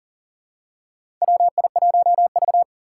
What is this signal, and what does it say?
Morse code, heard as a keyed tone, about 40 wpm, 700 Hz. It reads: WI1V
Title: USA Call Signs - 2x Character Spacing - 40wpm